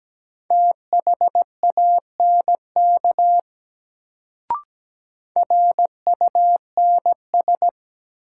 Morse code, heard as a keyed tone, about 17 wpm, 700 Hz. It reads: THANK RUNS
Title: Sentences from Top 600 Words - New Word Drill - 17wpm